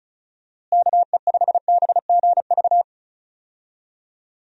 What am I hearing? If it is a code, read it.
Morse code, heard as a keyed tone, about 36 wpm, 700 Hz. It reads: KE5BGV
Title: USA Call Signs - 6 Characters - Repeated 3x - 35wpm